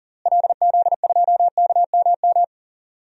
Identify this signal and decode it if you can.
Morse code, heard as a keyed tone, about 38 wpm, 700 Hz. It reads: LZ2KMM
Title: DXCC-100 Call Signs - 6 Characters - Repeated 2x - 40wpm